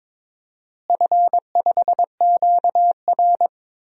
Morse code, heard as a keyed tone, about 22 wpm, 700 Hz. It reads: F5QR
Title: DXCC-100 Call Signs - 4 Characters - Repeated 2x - 22wpm